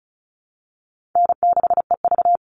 Morse code, heard as a keyed tone, about 34 wpm, 700 Hz. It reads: N6EV
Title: Easy Call Signs Repeated 2x - 35wpm